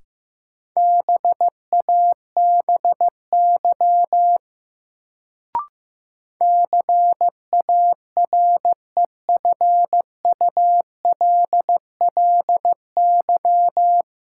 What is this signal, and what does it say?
Morse code, heard as a keyed tone, about 15 wpm, 700 Hz. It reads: BABY CAREFULLY
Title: Sentences from Top 1300 Words - New Word Drill - 15wpm